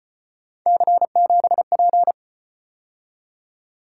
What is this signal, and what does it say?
Morse code, heard as a keyed tone, about 35 wpm, 700 Hz. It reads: C7P